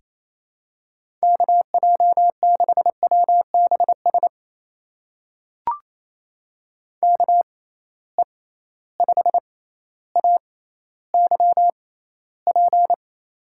Morse code, heard as a keyed tone, about 28 wpm, spaced as for 14 wpm, 700 Hz. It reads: KJ6WBS KE5AYP
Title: USA Call Signs - 6 Characters - 6x Character Spacing - 28wpm